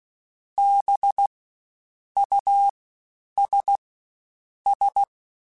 Morse code, heard as a keyed tone, about 16 wpm, spaced as for 8 wpm, 800 Hz. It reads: BUSS
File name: Special INGs-mp3s / Special INGs-mp3s / Fast ing @15